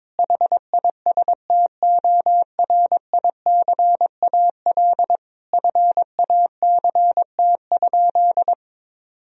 Morse code, heard as a keyed tone, about 21 wpm, 700 Hz. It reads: HISTORICAL FACT?
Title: Sentences from Top 1000 Words - Review Part 4 - 22wpm